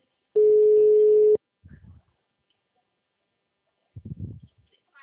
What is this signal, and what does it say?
Ringback tone (Europe, 425 Hz).